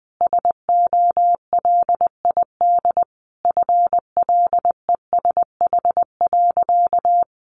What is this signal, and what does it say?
Morse code, heard as a keyed tone, about 19 wpm, 700 Hz. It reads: SOLID FLESH.
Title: A Sound Of Thunder 20wpm